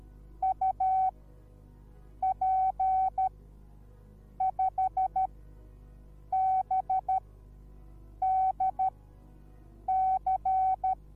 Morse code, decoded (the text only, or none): UP5BDC